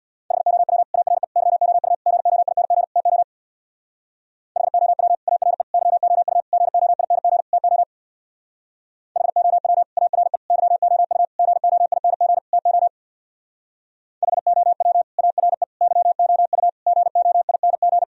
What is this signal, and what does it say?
Morse code, heard as a keyed tone, about 59 wpm, 700 Hz. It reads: HOW ARE YOU GOING TO HOW ARE YOU GOING TO HOW ARE YOU GOING TO HOW ARE YOU GOING
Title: Speed-Racing - Sets of 5 Words 50wpm